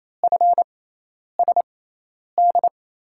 Morse code, heard as a keyed tone, about 28 wpm, spaced as for 10 wpm, 700 Hz. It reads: FSD
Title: USA Call Signs - 6 Characters - 6x Character Spacing - 28wpm